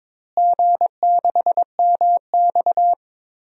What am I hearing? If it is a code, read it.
Morse code, heard as a keyed tone, about 21 wpm, 700 Hz. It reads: G6MX
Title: DXCC-100 Call Signs - 4 Characters - Repeated 3x - 22wpm